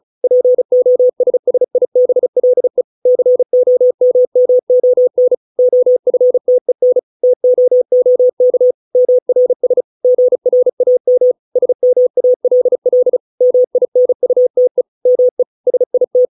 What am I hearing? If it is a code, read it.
Morse code, heard as a keyed tone, about 34 wpm, 500 Hz. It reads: POSSIBLE COMMON OFTEN TOOK MRS GRAM SMALL MINUTE ME SIT